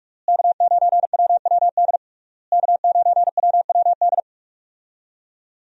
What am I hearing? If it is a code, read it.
Morse code, heard as a keyed tone, about 41 wpm, 700 Hz. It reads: K9WWD K9WWD
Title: USA Call Signs - 5 Characters - Repeated 3x - 45wpm